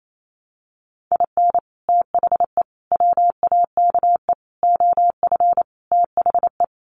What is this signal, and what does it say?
Morse code, heard as a keyed tone, about 28 wpm, 700 Hz. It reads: IN THE WAKE OF THE
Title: Sets of 5 Words 28wpm